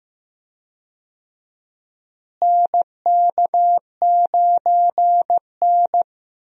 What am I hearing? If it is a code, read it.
Morse code, heard as a keyed tone, about 15 wpm, 700 Hz. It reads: NK9N
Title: USA Call Signs - 4 Characters - Repeated 3x - 15wpm